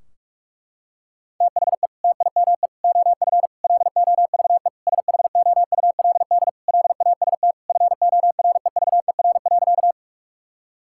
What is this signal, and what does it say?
Morse code, heard as a keyed tone, about 46 wpm, 700 Hz. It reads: THE TIME OF LOVE SHOULD LAST FOREVER.
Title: Sentences from Top 1600 Words 45wpm